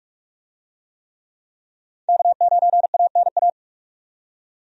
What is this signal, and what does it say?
Morse code, heard as a keyed tone, about 43 wpm, 700 Hz. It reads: K9ANA
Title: USA Call Signs - 5 Characters - 45wpm